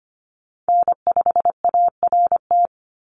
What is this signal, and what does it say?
Morse code, heard as a keyed tone, about 25 wpm, 700 Hz. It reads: N5ART
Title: USA Call Signs 25wpm